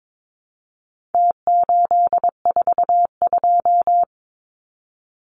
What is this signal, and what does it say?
Morse code, heard as a keyed tone, about 22 wpm, 700 Hz. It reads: T842
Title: Numbers - Contest Style - 4-Digit 22wpm